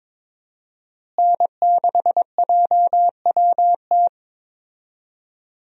Morse code, heard as a keyed tone, about 22 wpm, 700 Hz. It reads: N6JWT
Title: USA Call Signs - 5 Characters - Repeated 2x - 22wpm